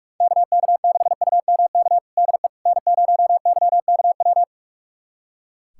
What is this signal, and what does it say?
Morse code, heard as a keyed tone, about 42 wpm, 700 Hz. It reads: KK6UMK DE N0YKW
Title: USA Callsign DE Callsign 45wpm